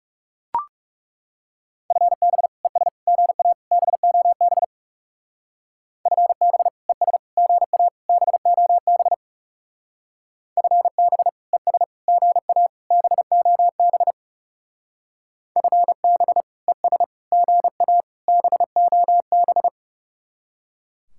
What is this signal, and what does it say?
Morse code, heard as a keyed tone, about 35 wpm, 700 Hz. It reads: FB ES GA BOB FB ES GA BOB FB ES GA BOB FB ES GA BOB
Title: Speed-Racing Common QSO Phrases Encore - 30wpm